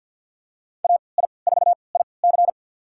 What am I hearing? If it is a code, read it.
Morse code, heard as a keyed tone, about 47 wpm, spaced as for 28 wpm, 700 Hz. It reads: AI4IC
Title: USA Call Signs - 5 Characters - 3x Character Spacing - 50wpm